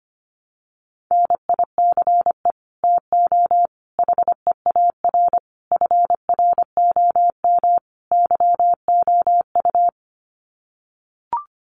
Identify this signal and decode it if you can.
Morse code, heard as a keyed tone, about 25 wpm, 700 Hz. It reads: NICE TO HEAR FROM YOU
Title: Common QSO Phrases 25wpm